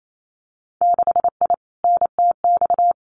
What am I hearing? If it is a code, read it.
Morse code, heard as a keyed tone, about 28 wpm, 700 Hz. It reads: 6I NTX
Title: Winter Field Day Exchanges 28wpm